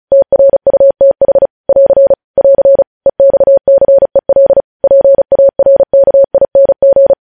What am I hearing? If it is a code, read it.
Morse code, heard as a keyed tone, about 36 wpm, 550 Hz. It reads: TRUTH <AR> <AR> EXCEL PARKING
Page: news headlines in morse code at 35 wpm